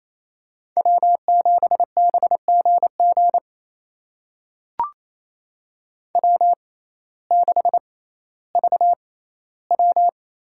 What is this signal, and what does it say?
Morse code, heard as a keyed tone, about 28 wpm, spaced as for 15 wpm, 700 Hz. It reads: W7BGG W6VW
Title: USA Call Signs - 6x Character Spacing - 28wpm